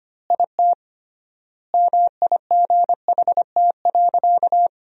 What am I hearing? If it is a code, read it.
Morse code, heard as a keyed tone, about 25 wpm, 700 Hz. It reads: IT MIGHT.
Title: 3x Word-Spacing - Sentences from Top 500 Words - 25wpm